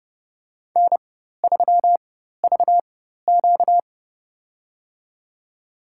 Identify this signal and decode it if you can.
Morse code, heard as a keyed tone, about 30 wpm, spaced as for 14 wpm, 700 Hz. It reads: N3VQ
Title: USA Call Signs - 4 Characters - 4x Character Spacing - 30wpm